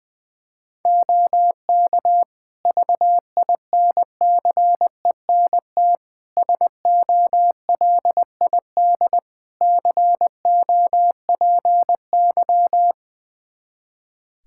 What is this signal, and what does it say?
Morse code, heard as a keyed tone, about 20 wpm, 700 Hz. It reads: OK VINCENT SOLID COPY
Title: Practice QSO - Exchange 3 - More Info - 20wpm